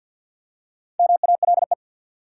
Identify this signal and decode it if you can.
Morse code, heard as a keyed tone, about 52 wpm, 700 Hz. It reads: MALE
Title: Four Letter Words - 50wpm